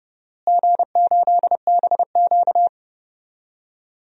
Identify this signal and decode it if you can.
Morse code, heard as a keyed tone, about 30 wpm, 700 Hz. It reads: G8BQ
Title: DXCC-100 Call Signs - 4 Characters - Repeated 3x - 30wpm